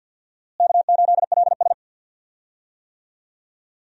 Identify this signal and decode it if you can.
Morse code, heard as a keyed tone, about 53 wpm, 690 Hz. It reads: K8LS
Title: USA Call Signs - 4 Characters - Repeated 2x - 50wpm